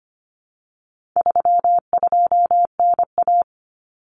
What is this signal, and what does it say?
Morse code, heard as a keyed tone, about 25 wpm, 700 Hz. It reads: 32NA
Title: Common Cut Numbers - 4-Digit 25wpm